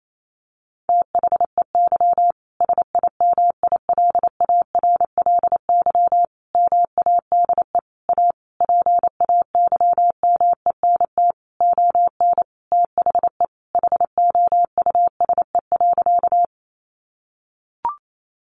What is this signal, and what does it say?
Morse code, heard as a keyed tone, about 28 wpm, 700 Hz. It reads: THEY SIMILARLY MADE A PAYMENT ON THE HOUSE.
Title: Sentences from Top 1400 Words 28wpm